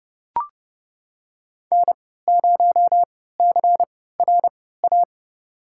Morse code, heard as a keyed tone, about 30 wpm, spaced as for 17 wpm, 700 Hz. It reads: N0CRA
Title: USA Call Signs - 5 Characters - 3x Character Spacing - 30wpm